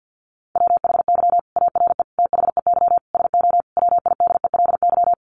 Morse code, heard as a keyed tone, about 50 wpm, 700 Hz. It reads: WHY ARE THEY SO WIDELY
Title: Sentences from Top 1200 Words 50wpm